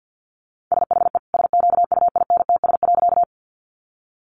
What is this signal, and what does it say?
Morse code, heard as a keyed tone, about 50 wpm, 700 Hz. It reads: SHE SQUINTS.
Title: Additional Easy Phrases 50wpm